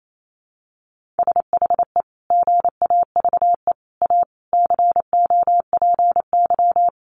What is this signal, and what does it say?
Morse code, heard as a keyed tone, about 28 wpm, 700 Hz. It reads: SHE GAVE A COPY
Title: Sentences from Top 1400 Words 28wpm